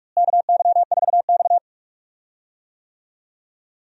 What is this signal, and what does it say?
Morse code, heard as a keyed tone, about 42 wpm, 700 Hz. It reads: KY4X
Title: USA Call Signs - 4 Characters - Repeated 3x - 45wpm